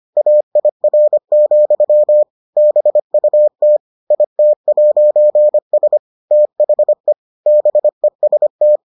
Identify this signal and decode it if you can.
Morse code, heard as a keyed tone, about 25 wpm, 600 Hz. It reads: AIR, BUT IT'S THE BEST